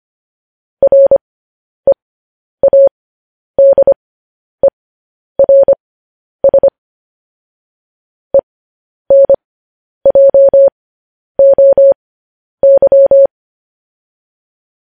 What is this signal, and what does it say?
Morse code, heard as a keyed tone, about 25 wpm, spaced as for 10 wpm, 550 Hz. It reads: READERS ENJOY